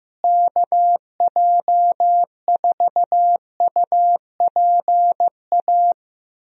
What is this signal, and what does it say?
Morse code, heard as a keyed tone, about 15 wpm, 700 Hz. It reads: KJ4UPA
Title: USA Call Signs - 6 Characters - Repeated 2x - 15wpm